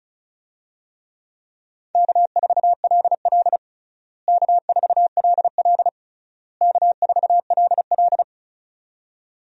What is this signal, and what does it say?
Morse code, heard as a keyed tone, about 36 wpm, 700 Hz. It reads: K4LL K4LL K4LL
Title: USA Call Signs - 4 Characters - Repeated 3x - 35wpm